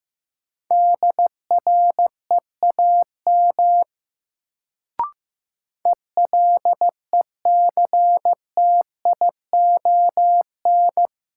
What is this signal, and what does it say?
Morse code, heard as a keyed tone, about 15 wpm, 700 Hz. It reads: DREAM ELECTION
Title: Sentences from Top 1500 Words - New Word Drill - 15wpm